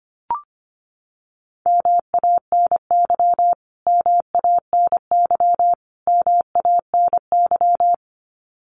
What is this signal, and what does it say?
Morse code, heard as a keyed tone, about 25 wpm, 700 Hz. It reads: MANY MANY MANY
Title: Top 500 Words Repeated 25wpm